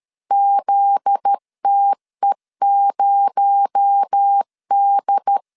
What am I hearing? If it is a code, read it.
Morse code, heard as a keyed tone, about 13 wpm, 800 Hz. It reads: ZTE0D
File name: CW / DET / 1 9 1500C S C 12wpm 3811